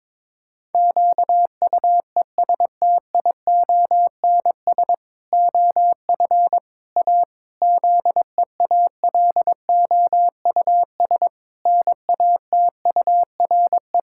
Morse code, heard as a keyed tone, about 22 wpm, 700 Hz. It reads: QUESTIONS OF A ZEALOUS NATURE
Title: Pangram Challenge v2 - 22wpm